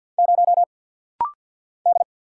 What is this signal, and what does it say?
Morse code, heard as a keyed tone, about 50 wpm, 700 Hz. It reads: 0 D
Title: Single Letter-Number Alt - Rapid-Fire 50wpm